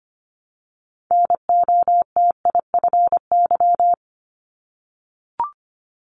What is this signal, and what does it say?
Morse code, heard as a keyed tone, about 25 wpm, 700 Hz. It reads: NOTIFY
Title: Multi-letter Phonemes Repeated 3x 25wpm